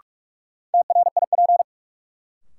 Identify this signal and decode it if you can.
Morse code, heard as a keyed tone, about 46 wpm, 700 Hz. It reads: TRIP
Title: Sentences from Top 1000 Words - Review Part 4 - New Word Drill - 45wpm